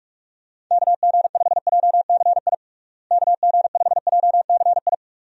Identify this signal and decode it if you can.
Morse code, heard as a keyed tone, about 42 wpm, 700 Hz. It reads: KG5JCI KG5JCI
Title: USA Call Signs - 6 Characters - Repeated 3x - 45wpm